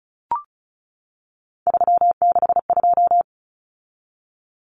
Morse code, heard as a keyed tone, about 34 wpm, 700 Hz. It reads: 362
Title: Common Cut Numbers - 3-Digit 35wpm